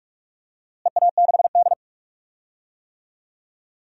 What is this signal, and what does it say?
Morse code, heard as a keyed tone, about 46 wpm, 700 Hz. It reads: EA6D
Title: DXCC-100 Call Signs - 4 Characters - Repeated 3x - 45wpm